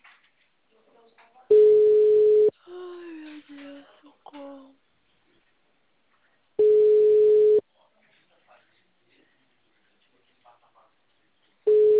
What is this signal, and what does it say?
Ringback tone (Europe, 425 Hz), with speech.
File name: out- -2013- - - .12497